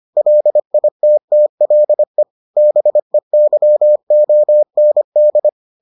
Morse code, heard as a keyed tone, about 25 wpm, 600 Hz. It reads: LITTLE BEYOND